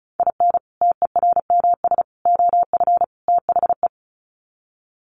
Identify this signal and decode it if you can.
Morse code, heard as a keyed tone, about 34 wpm, 700 Hz. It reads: IN TERMS OF THE
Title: Sets of 4 Words 35wpm